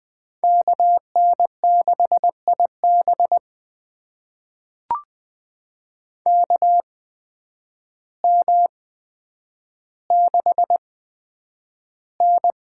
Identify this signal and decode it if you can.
Morse code, heard as a keyed tone, about 20 wpm, spaced as for 9 wpm, 700 Hz. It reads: KN6IB KM6N
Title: USA Call Signs - 8x Character Spacing - 20wpm